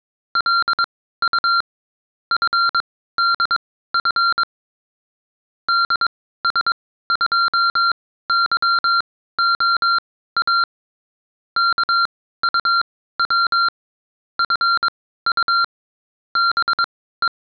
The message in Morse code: LU FDF DS2YOA KUW FU BE